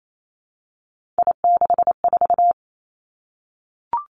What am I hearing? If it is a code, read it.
Morse code, heard as a keyed tone, about 28 wpm, 700 Hz. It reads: I64
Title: Three Challenging Letter-Number Combinations 28wpm